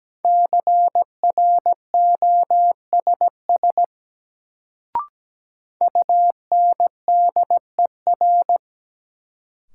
Morse code, heard as a keyed tone, about 17 wpm, 700 Hz. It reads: CROSS UNDER
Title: Sentences from Top 300 Words - New Word Drill - 17wpm